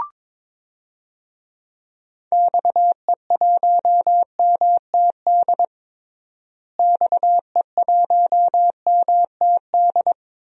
Morse code, heard as a keyed tone, about 21 wpm, 700 Hz. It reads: XE1MTD XE1MTD
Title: DXCC-100 Call Signs - 6 Characters - Repeated 2x - 22wpm